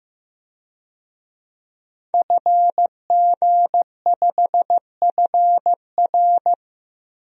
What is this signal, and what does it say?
Morse code, heard as a keyed tone, about 16 wpm, 700 Hz. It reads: FG5FR
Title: DXCC-100 Call Signs - 5 Characters - Repeated 2x - 15wpm